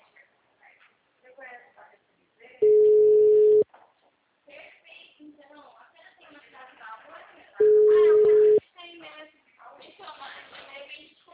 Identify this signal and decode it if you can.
Ringback tone (Europe, 425 Hz), with speech.
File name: out- -2005- - - .42901